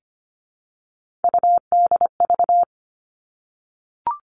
Morse code, heard as a keyed tone, about 25 wpm, 700 Hz. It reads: UDV
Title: All Cut Numbers - 3-Digit 25wpm